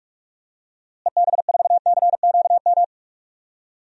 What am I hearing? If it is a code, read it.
Morse code, heard as a keyed tone, about 46 wpm, 700 Hz. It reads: EB4CQM
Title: DXCC-100 Call Signs - 6 Characters - Repeated 3x - 45wpm